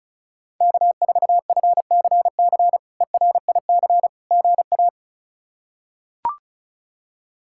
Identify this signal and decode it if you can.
Morse code, heard as a keyed tone, about 34 wpm, 700 Hz. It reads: K4FCC ERIC GA
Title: K1USN SST QSO Practice - 35wpm